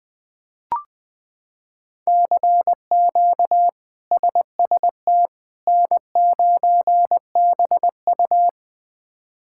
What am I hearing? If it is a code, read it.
Morse code, heard as a keyed tone, about 21 wpm, 700 Hz. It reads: CQ SST N9BU
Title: K1USN SST QSO Practice - 20wpm